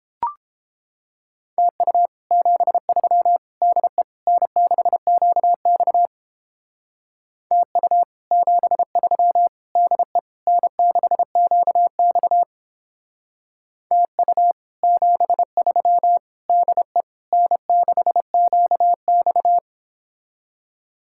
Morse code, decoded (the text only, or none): TU 73 DE N6QX TU 73 DE N6QX TU 73 DE N6QX